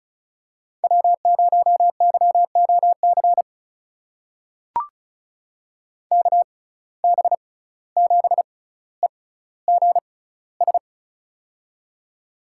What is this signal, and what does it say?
Morse code, heard as a keyed tone, about 36 wpm, spaced as for 17 wpm, 700 Hz. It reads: W0YOC KB7EGS
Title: USA Call Signs - 6x Character Spacing - 35wpm